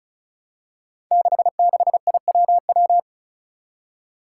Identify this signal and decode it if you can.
Morse code, heard as a keyed tone, about 36 wpm, 700 Hz. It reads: 66IWW
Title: Five Challenging Letter-Number Combinations 35wpm